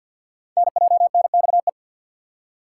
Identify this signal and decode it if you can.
Morse code, heard as a keyed tone, about 47 wpm, 700 Hz. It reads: NJNXE
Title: Code Groups - 5 Letters - 50wpm